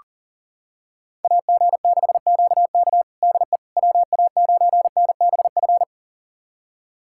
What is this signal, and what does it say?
Morse code, heard as a keyed tone, about 40 wpm, 700 Hz. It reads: AG6QK DE WA9NBF
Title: USA Callsign DE Callsign 40wpm